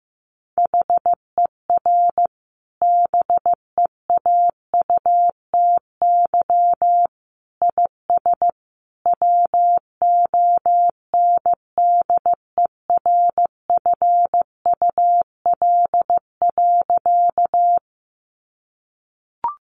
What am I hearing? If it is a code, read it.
Morse code, heard as a keyed tone, about 15 wpm, 700 Hz. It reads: HER BEAUTY IS WONDERFUL.
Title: Sentences from Top 1100 Words 15wpm